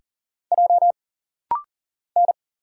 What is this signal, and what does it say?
Morse code, heard as a keyed tone, about 40 wpm, 700 Hz. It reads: J N
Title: Single Letter-Number Alt - Rapid-Fire 40wpm